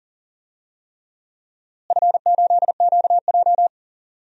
Morse code, heard as a keyed tone, about 40 wpm, 700 Hz. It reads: F8QJ
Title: DXCC-100 Call Signs - 4 Characters - Repeated 3x - 40wpm